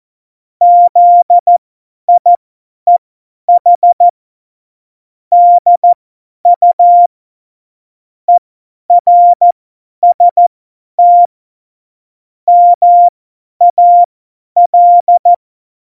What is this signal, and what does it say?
Morse code, heard as a keyed tone, about 14 wpm, spaced as for 10 wpm, 700 Hz. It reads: ZIEH DU ERST MAL
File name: Morsekurs / Witze / Witz 15. Dezember 2020, 50 F70